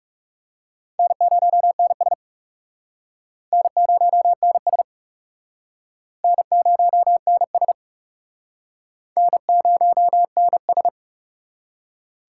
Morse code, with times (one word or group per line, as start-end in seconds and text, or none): 1.0-2.1 N0NS
3.5-4.8 N0NS
6.2-7.7 N0NS
9.2-10.9 N0NS